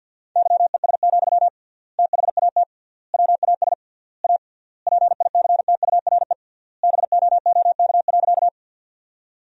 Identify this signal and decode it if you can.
Morse code, heard as a keyed tone, about 50 wpm, 700 Hz. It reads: YES, THAT WAS A PICTURE BOOK.
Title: 3x Word-Spacing - Sentences from Top 500 Words - 50wpm